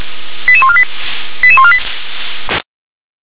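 Five-tone selective call (ZVEI), heard as quick runs of five tones, tones of about 80 ms. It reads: 88148 88148
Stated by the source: ZVEI five-tone selective call received on an SDR.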